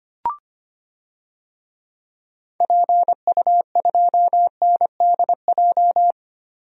Morse code, heard as a keyed tone, about 25 wpm, 700 Hz. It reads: PU2NDJ
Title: DXCC-100 Call Signs - 6 Characters - 25wpm